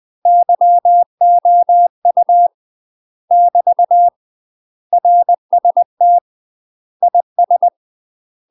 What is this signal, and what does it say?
Morse code, heard as a keyed tone, about 20 wpm, 700 Hz. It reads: YOU <BT> RST IS
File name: UK MASTER QSOs TRAINING / 20WPM / UK MASTER QSOs 0066